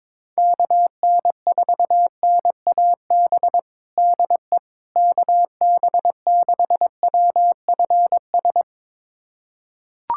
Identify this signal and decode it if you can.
Morse code, heard as a keyed tone, about 22 wpm, 700 Hz. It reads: KN4NAB DE KB6WFS
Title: USA Callsign DE Callsign - 6 Characters - 22wpm